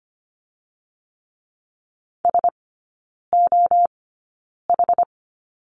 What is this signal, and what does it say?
Morse code, heard as a keyed tone, about 25 wpm, spaced as for 9 wpm, 700 Hz. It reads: SOH
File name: Lesson 17e - Letter u - Three Characters 25wpm